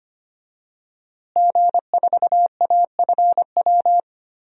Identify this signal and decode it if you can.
Morse code, heard as a keyed tone, about 25 wpm, 700 Hz. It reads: G4AFW
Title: DXCC-100 Call Signs - 5 Characters - Repeated 3x - 25wpm